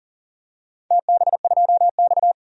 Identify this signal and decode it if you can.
Morse code, heard as a keyed tone, about 40 wpm, 700 Hz. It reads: TB2X
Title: DXCC-100 Call Signs - 4 Characters - 40wpm